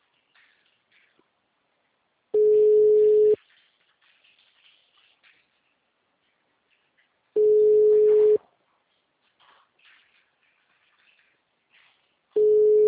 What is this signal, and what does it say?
Ringback tone (Europe, 425 Hz).